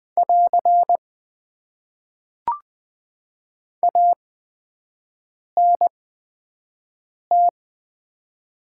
Morse code, heard as a keyed tone, about 21 wpm, spaced as for 5 wpm, 700 Hz. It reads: <AR> ANT
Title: Top 90 QSO Elements - 8x Character Spacing - 20wpm